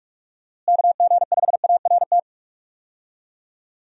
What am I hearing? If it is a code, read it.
Morse code, heard as a keyed tone, about 45 wpm, 700 Hz. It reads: KG5ART
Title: USA Call Signs - 6 Characters - Repeated 2x - 45wpm